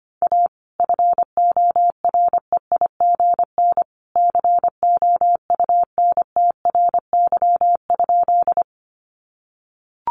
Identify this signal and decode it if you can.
Morse code, heard as a keyed tone, about 25 wpm, 700 Hz. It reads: A FOREIGN COUNTRY?
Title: Sentences from Top 1400 Words 25wpm